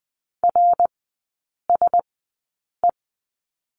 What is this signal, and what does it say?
Morse code, heard as a keyed tone, about 20 wpm, spaced as for 8 wpm, 700 Hz. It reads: RSE
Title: 053 Lesson 10e - Letter r - Three Characters 20wpm